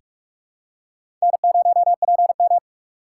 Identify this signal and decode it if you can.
Morse code, heard as a keyed tone, about 43 wpm, 700 Hz. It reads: N0PM
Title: USA Call Signs - 4 Characters - 2x Character Spacing - 45wpm